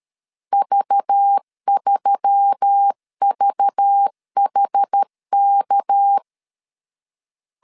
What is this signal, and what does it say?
Morse code, heard as a keyed tone, about 13 wpm, 800 Hz. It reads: V3VHK
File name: CW / DET / 1 6 1500C S A 12wpm 3440